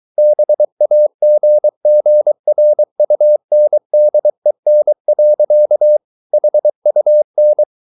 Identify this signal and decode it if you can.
Morse code, heard as a keyed tone, about 23 wpm, 600 Hz. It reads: BAGGRUNDEN. HUN
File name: Ferie QRS 23wpm